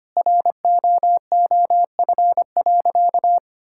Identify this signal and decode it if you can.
Morse code, heard as a keyed tone, about 25 wpm, 700 Hz. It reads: ROOF.
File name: practice / 25wpm / All-Age Appropriate Jokes - 25wpm